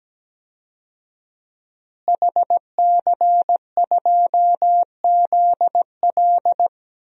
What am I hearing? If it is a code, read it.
Morse code, heard as a keyed tone, about 17 wpm, 700 Hz. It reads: HC2ZL
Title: DXCC-100 Call Signs - 5 Characters - Repeated 2x - 17wpm